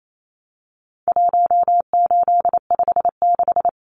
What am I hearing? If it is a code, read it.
Morse code, heard as a keyed tone, about 28 wpm, 700 Hz. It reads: 1856
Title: Numbers - 4-Digit 28wpm